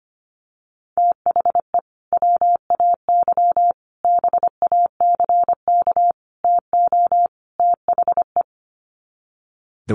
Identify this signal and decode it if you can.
Morse code, heard as a keyed tone, about 25 wpm, 700 Hz. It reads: THE WAY BACK TO THE
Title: Sets of 5 Words Encore2 25wpm